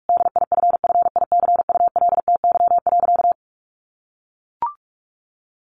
Morse code, heard as a keyed tone, about 44 wpm, 700 Hz. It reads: DIFFICULTY.
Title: Sentences from Top 1500 Words 45wpm